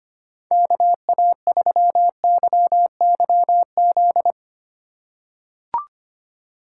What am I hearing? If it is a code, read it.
Morse code, heard as a keyed tone, about 25 wpm, 700 Hz. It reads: KA3YYZ
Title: USA Call Signs - 6 Characters - 2x Character Spacing - 25wpm